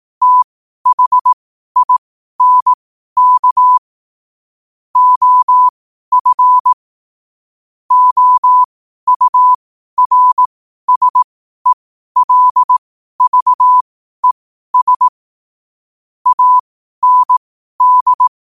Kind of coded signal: Morse code